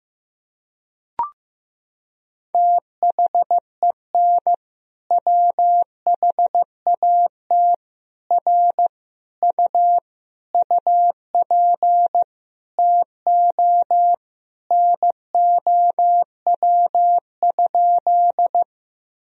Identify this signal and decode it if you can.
Morse code, heard as a keyed tone, about 15 wpm, 700 Hz. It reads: THEN WHAT R U UP TO NOW?
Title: Sentences from Top 100 Words and Abbreviations 15wpm